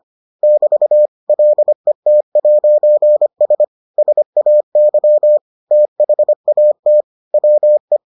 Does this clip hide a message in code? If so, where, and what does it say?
Yes: Morse code from 0.4 to 8.0 s, heard as a keyed tone, about 25 wpm, 600 Hz. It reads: <BT> LET'S SAY THAT WE